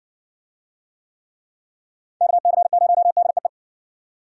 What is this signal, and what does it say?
Morse code, heard as a keyed tone, about 58 wpm, 700 Hz. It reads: KC0DEE